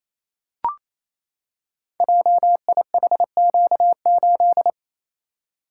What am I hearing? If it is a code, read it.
Morse code, heard as a keyed tone, about 28 wpm, 700 Hz. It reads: JIHQ8
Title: Code Groups - 5 Letter:Numbers - 28wpm